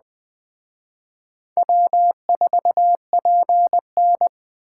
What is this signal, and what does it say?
Morse code, heard as a keyed tone, about 20 wpm, 700 Hz. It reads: W4PN
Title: Speed Racing - USA 1x2 Call Signs 17wpm